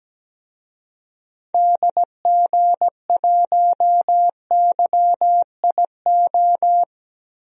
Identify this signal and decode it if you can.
Morse code, heard as a keyed tone, about 17 wpm, 700 Hz. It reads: DG1YIO
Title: DXCC-100 Call Signs - 6 Characters - Repeated 2x - 17wpm